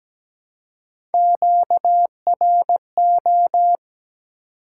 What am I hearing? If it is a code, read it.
Morse code, heard as a keyed tone, about 17 wpm, 700 Hz. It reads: QRO
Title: Top 180 QSO Elements 17wpm